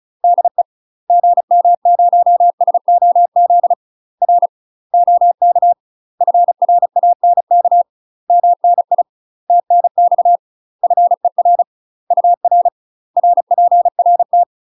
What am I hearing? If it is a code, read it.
Morse code, heard as a keyed tone, about 35 wpm, 700 Hz. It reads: DE GM0SOZ R OK FRANK MNI TNX FER UR RPRT
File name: UK MASTER QSOs TRAINING / 35WPM / UK MASTER QSOs 0010